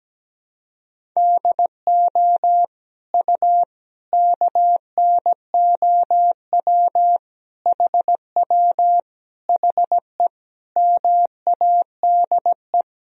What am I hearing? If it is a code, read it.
Morse code, heard as a keyed tone, about 17 wpm, 700 Hz. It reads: DO U KNOW HW HE MADE
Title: Sentences from Top 300 Words and Abbreviations 17wpm